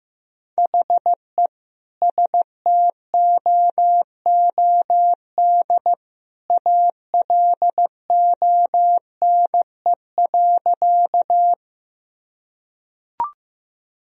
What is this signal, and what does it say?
Morse code, heard as a keyed tone, about 15 wpm, 700 Hz. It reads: HE STOOD ALONE.
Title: Sentences from Top 1000 Words - Review Part 2 - 15wpm